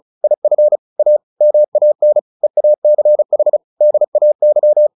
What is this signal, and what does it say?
Morse code, heard as a keyed tone, about 34 wpm, 600 Hz. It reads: IF A MAN EACH DAY
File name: the old man and the sea 2 35wpm